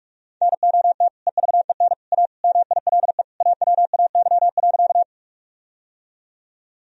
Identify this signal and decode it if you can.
Morse code, heard as a keyed tone, about 47 wpm, 700 Hz. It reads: NOT EVEN A MILE AWAY.
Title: Sentences from Top 1000 Words - Review Part 1 - 45wpm